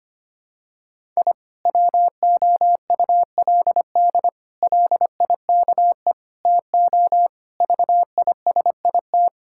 Morse code, heard as a keyed tone, about 25 wpm, 700 Hz. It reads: I WOULD LIKE TO VISIT
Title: Sentences from Top 1000 Words - Review Part 6 - 25wpm